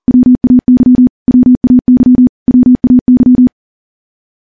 Morse code, heard as a keyed tone, about 40 wpm, 250 Hz. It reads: WAY WAY WAY
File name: morse / morse-training-40WPM